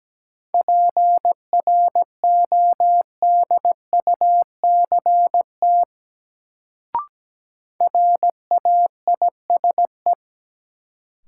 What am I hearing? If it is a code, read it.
Morse code, heard as a keyed tone, about 17 wpm, 700 Hz. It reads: PRODUCT RAISE
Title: Sentences from Top 1400 Words - New Word Drill Alt - 17wpm